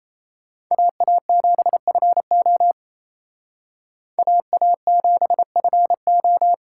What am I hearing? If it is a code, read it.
Morse code, heard as a keyed tone, about 34 wpm, 700 Hz. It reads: AA7FO AA7FO
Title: Speed Racing - USA 2x2 Call Signs 28wpm